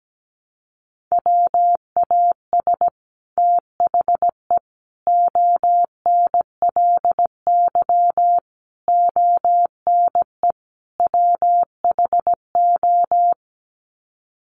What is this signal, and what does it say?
Morse code, heard as a keyed tone, about 17 wpm, 700 Hz. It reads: WAS THE ONLY ONE WHO
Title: Sets of 5 Words Encore2 17wpm